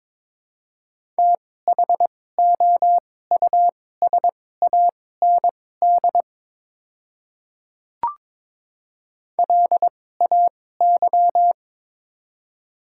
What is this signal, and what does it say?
Morse code, heard as a keyed tone, about 22 wpm, spaced as for 16 wpm, 700 Hz. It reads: THOUSAND LAY
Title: 2x Character Spacing - Top 500 Words - 22wpm